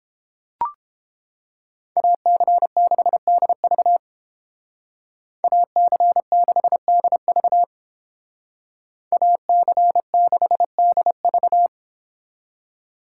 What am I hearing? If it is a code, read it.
Morse code, heard as a keyed tone, about 28 wpm, 700 Hz. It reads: AC6DV AC6DV AC6DV